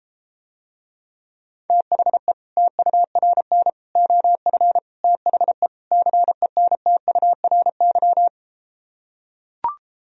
Morse code, heard as a keyed tone, about 33 wpm, 700 Hz. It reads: THE TURN OF THE CENTURY